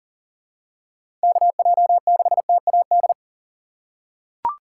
Morse code, heard as a keyed tone, about 40 wpm, 690 Hz. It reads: KJ6TAD
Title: USA Call Signs - 4x Character Spacing - 40wpm